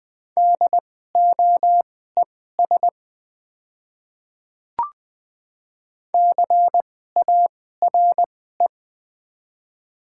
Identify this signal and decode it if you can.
Morse code, heard as a keyed tone, about 20 wpm, spaced as for 14 wpm, 700 Hz. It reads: DOES CARE
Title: 2x Character Spacing - Top 300 Words - 20wpm